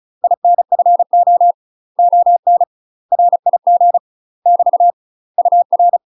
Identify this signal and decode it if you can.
Morse code, heard as a keyed tone, about 33 wpm, 700 Hz. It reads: INFO ON RIG <BT> UR
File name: UK MASTER QSOs TRAINING / 35WPM / UK MASTER QSOs 0033